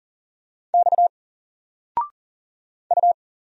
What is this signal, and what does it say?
Morse code, heard as a keyed tone, about 40 wpm, 700 Hz. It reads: X U
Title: Single Letter-Number Alt 40wpm